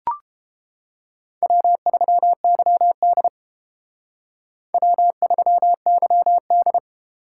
Morse code, heard as a keyed tone, about 30 wpm, 700 Hz. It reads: W3YD W3YD